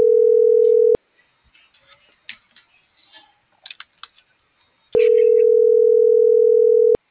Ringback tone (North America), 440 + 480 Hz.
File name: CENTRAL GENERAL / out michelle.carrera